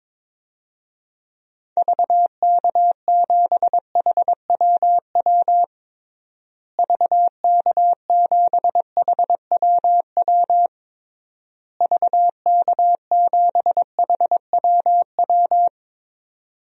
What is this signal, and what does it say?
Morse code, heard as a keyed tone, about 22 wpm, 700 Hz. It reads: VK7HWW VK7HWW VK7HWW
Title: DXCC-100 Call Signs - 6 Characters - Repeated 3x - 22wpm